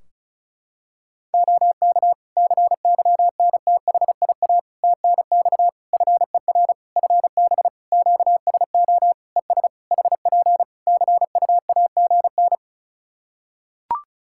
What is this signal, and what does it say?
Morse code, heard as a keyed tone, about 36 wpm, 700 Hz. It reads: OK CYNTHIA TNX FER FB QSO ES HP CUAGN
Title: K1USN SST QSO Practice 35wpm